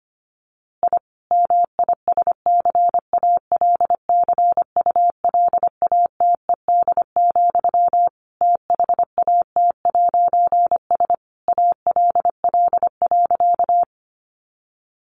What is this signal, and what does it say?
Morse code, heard as a keyed tone, about 25 wpm, 700 Hz. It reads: I MISCALCULATED, THAT'S ALL.
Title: A Sound Of Thunder 25wpm